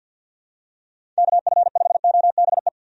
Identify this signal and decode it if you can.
Morse code, heard as a keyed tone, about 51 wpm, 700 Hz. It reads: KF5OBE
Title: USA Call Signs - 6 Characters - 50wpm